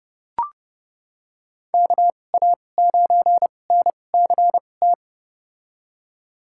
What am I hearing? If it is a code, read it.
Morse code, heard as a keyed tone, about 30 wpm, spaced as for 22 wpm, 700 Hz. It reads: KA9NCT
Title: USA Call Signs - 2x Character Spacing - 30wpm